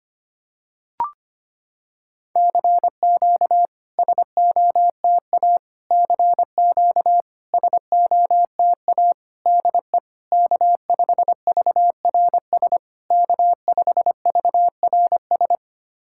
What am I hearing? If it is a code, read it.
Morse code, heard as a keyed tone, about 25 wpm, 700 Hz. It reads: CQ SOTA CQ SOTA DE K5VRS K5VRS
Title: SOTA QSO Practice 25wpm